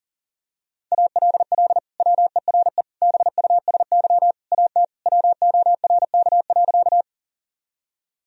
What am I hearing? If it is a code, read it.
Morse code, heard as a keyed tone, about 40 wpm, 700 Hz. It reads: ALL WERE BUSY AT WORK.
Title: Sentences from Top 1000 Words - Review Part 3 - 40wpm